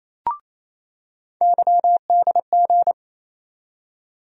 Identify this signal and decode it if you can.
Morse code, heard as a keyed tone, about 28 wpm, 700 Hz. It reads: YDG
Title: USA Call Sign Postfixes - 3 Letters 28wpm